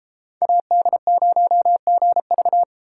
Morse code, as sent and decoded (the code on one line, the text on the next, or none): .- -.. ----- --. ...-
AD0GV